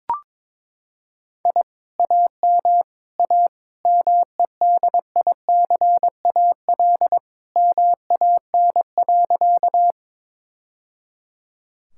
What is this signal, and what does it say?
Morse code, heard as a keyed tone, about 21 wpm, 700 Hz. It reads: I AM A MEDICAL MAN.
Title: Sentences from Top 1000 Words - Review Part 5 - 22wpm